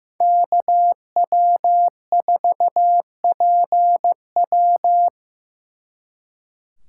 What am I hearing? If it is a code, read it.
Morse code, heard as a keyed tone, about 18 wpm, 700 Hz. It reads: KW4PW
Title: Speed Racing - USA 2x2 Call Signs 15wpm